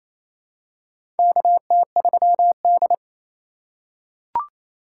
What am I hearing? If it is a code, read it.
Morse code, heard as a keyed tone, about 28 wpm, 700 Hz. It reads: KT3D
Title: USA Call Signs - 4 Characters - 4x Character Spacing - 28wpm